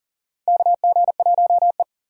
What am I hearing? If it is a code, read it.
Morse code, heard as a keyed tone, about 40 wpm, 700 Hz. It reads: KG1E
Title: USA Call Signs - 4 Characters - 6x Character Spacing - 40wpm